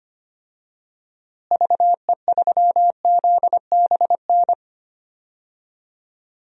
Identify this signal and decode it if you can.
Morse code, heard as a keyed tone, about 25 wpm, 700 Hz. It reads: VE3ZBN
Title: DXCC-100 Call Signs - 6 Characters - Repeated 2x - 25wpm